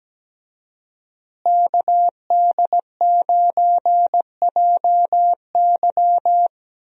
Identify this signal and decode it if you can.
Morse code, heard as a keyed tone, about 17 wpm, 700 Hz. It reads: KD9JY
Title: USA Call Signs - 5 Characters - Repeated 2x - 17wpm